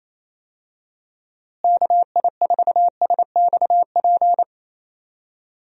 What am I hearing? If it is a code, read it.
Morse code, heard as a keyed tone, about 28 wpm, 700 Hz. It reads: KI4SXP
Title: USA Call Signs - 6 Characters - Repeated 3x - 28wpm